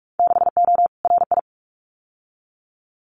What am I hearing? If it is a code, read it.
Morse code, heard as a keyed tone, about 46 wpm, 700 Hz. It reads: 6O RI
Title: Winter Field Day Exchanges Encore 45wpm